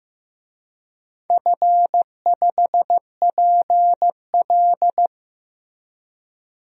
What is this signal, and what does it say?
Morse code, heard as a keyed tone, about 15 wpm, 700 Hz. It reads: F5PL
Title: DXCC-100 Call Signs - 4 Characters - Repeated 2x - 15wpm